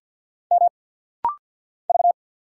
Morse code, heard as a keyed tone, about 50 wpm, 700 Hz. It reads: M V
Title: Single Letter-Number Alt - Rapid-Fire 50wpm